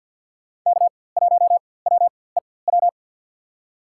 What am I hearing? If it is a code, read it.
Morse code, heard as a keyed tone, about 50 wpm, spaced as for 23 wpm, 700 Hz. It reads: K1WEW
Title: USA Call Signs - 5 Characters - 4x Character Spacing - 50wpm